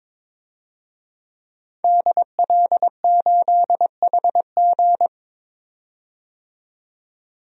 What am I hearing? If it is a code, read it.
Morse code, heard as a keyed tone, about 21 wpm, 700 Hz. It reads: DL8HG
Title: DXCC-100 Call Signs - 5 Characters - 22wpm